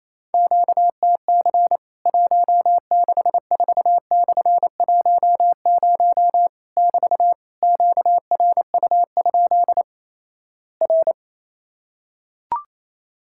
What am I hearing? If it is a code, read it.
Morse code, heard as a keyed tone, about 28 wpm, 700 Hz. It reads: QTC 164/10 <BT> QRU? R
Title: Worked All Europe DX Contest - QTC Exchange - 28wpm